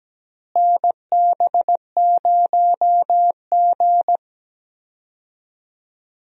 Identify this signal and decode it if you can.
Morse code, heard as a keyed tone, about 17 wpm, 700 Hz. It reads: NB0G
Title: USA Call Signs - 4 Characters - Repeated 3x - 17wpm